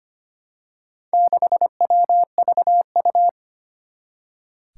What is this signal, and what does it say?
Morse code, heard as a keyed tone, about 25 wpm, 700 Hz. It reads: 6WVU
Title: Four Challenging Letter-Number Combinations 25wpm